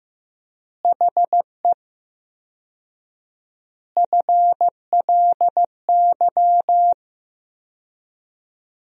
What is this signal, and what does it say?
Morse code, heard as a keyed tone, about 16 wpm, 700 Hz. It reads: HE FLY
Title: 4x Word-Spacing - Sentences from Top 500 Words - 15wpm